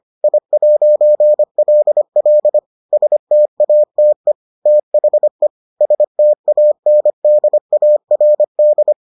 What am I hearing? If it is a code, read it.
Morse code, heard as a keyed tone, about 25 wpm, 600 Hz. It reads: I'LL STATE THE STANDARD